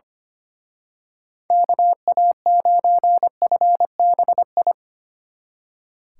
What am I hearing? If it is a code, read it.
Morse code, heard as a keyed tone, about 29 wpm, 700 Hz. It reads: KA9FBI
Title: Speed Racing - USA 2x3 Call Signs 25wpm